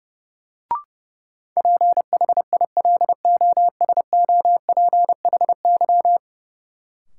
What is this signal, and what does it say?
Morse code, heard as a keyed tone, about 32 wpm, 700 Hz. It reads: PHILOSOPHY
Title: Sentences from Top 900 Words - New Word Drill - 30wpm